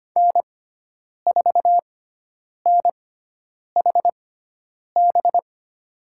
Morse code, heard as a keyed tone, about 25 wpm, spaced as for 9 wpm, 700 Hz. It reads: N4NHB
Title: USA Call Signs - 5 Characters - 6x Character Spacing - 25wpm